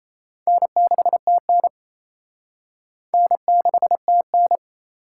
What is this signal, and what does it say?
Morse code, heard as a keyed tone, about 35 wpm, 700 Hz. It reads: N6TN N6TN